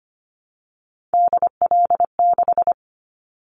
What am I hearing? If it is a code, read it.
Morse code, heard as a keyed tone, about 25 wpm, 700 Hz. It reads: DL6
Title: Callsign Prefixes for Top 50 DXCC - 25wpm